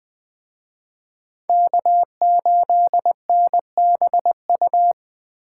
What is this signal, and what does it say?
Morse code, heard as a keyed tone, about 21 wpm, 700 Hz. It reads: K8NBU
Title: Speed Racing - USA 1x3 Call Signs 17wpm